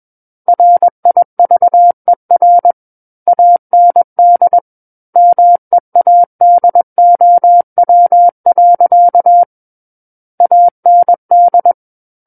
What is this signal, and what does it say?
Morse code, heard as a keyed tone, about 21 wpm, 700 Hz. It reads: RIVER AND MEADOW. AND
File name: Book One / 12-Chapter 12 - 21 WPM